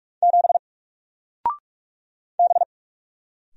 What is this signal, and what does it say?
Morse code, heard as a keyed tone, about 46 wpm, 700 Hz. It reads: 7 B
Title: Single Character (Letter:Number:Punctuation:Prosign) 45wpm